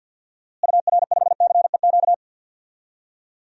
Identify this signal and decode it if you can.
Morse code, heard as a keyed tone, about 53 wpm, 700 Hz. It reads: UR5CEQ
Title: DXCC-100 Call Signs - 6 Characters - Repeated 2x - 50wpm